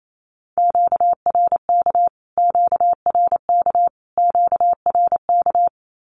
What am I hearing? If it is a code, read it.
Morse code, heard as a keyed tone, about 28 wpm, 700 Hz. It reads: QRK QRK QRK
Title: Q-Codes Repeated 3x 28wpm